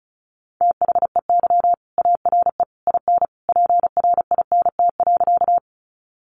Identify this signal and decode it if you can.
Morse code, heard as a keyed tone, about 34 wpm, 700 Hz. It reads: THEY ARE IN PRINT.
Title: Sentences from Top 1400 Words 35wpm